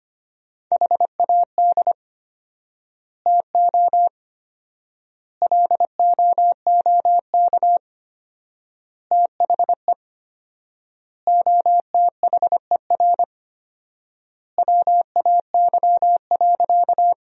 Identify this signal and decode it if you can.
Morse code, heard as a keyed tone, about 25 wpm, 700 Hz. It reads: HAD TO LOOK THE OTHER WAY.
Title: 4x Word-Spacing - Sentences from Top 100 Words - 25wpm